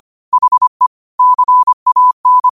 Morse code, heard as a keyed tone, about 25 wpm, 1000 Hz. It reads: HE CAN